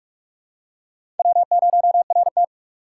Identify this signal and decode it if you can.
Morse code, heard as a keyed tone, about 41 wpm, 700 Hz. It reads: W0RT
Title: USA Call Signs - 4 Characters - 4x Character Spacing - 45wpm